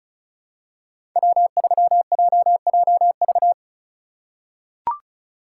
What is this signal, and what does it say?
Morse code, heard as a keyed tone, about 36 wpm, 700 Hz. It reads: W3JJV
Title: USA Call Signs - 5 Characters - 8x Character Spacing - 35wpm